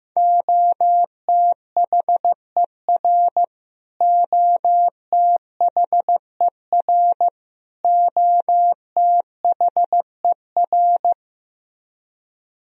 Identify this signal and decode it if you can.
Morse code, heard as a keyed tone, about 15 wpm, 700 Hz. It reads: OTHER OTHER OTHER
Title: Top 100 Words Repeated 15wpm